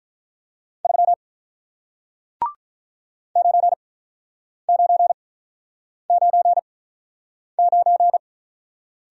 Morse code, heard as a keyed tone, about 48 wpm, spaced as for 9 wpm, 700 Hz. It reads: 3 9999